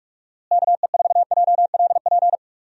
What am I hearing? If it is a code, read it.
Morse code, heard as a keyed tone, about 46 wpm, 700 Hz. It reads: KE4JLP
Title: USA Call Signs - 6 Characters - Repeated 3x - 45wpm